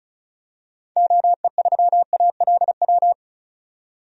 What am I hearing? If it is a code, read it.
Morse code, heard as a keyed tone, about 34 wpm, 700 Hz. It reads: OE3ALW
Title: DXCC-100 Call Signs - 6 Characters - 35wpm